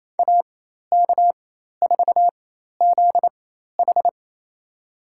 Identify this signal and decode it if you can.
Morse code, heard as a keyed tone, about 28 wpm, spaced as for 13 wpm, 700 Hz. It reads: AK4ZH
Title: USA Call Signs - 5 Characters - 4x Character Spacing - 28wpm